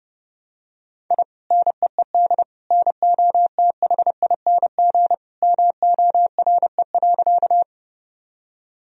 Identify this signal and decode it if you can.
Morse code, heard as a keyed tone, about 30 wpm, 700 Hz. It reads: I NEED NOTHING MORE.
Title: Sentences from Top 1000 Words - Review Part 1 - 30wpm